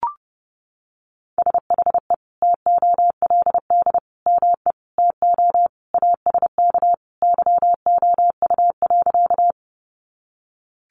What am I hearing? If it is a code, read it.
Morse code, heard as a keyed tone, about 28 wpm, 700 Hz. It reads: SHE TOLD ME TO ASK YOU.
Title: Sentences from Top 400 Words 30wpm